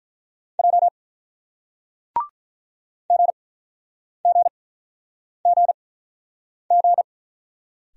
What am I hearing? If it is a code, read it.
Morse code, heard as a keyed tone, about 45 wpm, spaced as for 9 wpm, 700 Hz. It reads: J GGGG